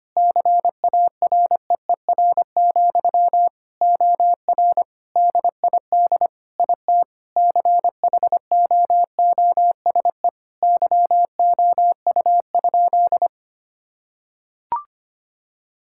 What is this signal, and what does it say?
Morse code, heard as a keyed tone, about 25 wpm, 700 Hz. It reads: CAREER, OR DID IT CHOOSE YOU?
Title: Sentences from Top 1000 Words - Review Part 6 - 25wpm